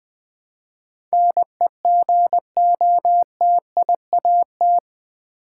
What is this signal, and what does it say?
Morse code, heard as a keyed tone, about 20 wpm, 700 Hz. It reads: NEGOTIAT
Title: Build A Word - 4 Letters At A Time - 11-Letter Words - 20wpm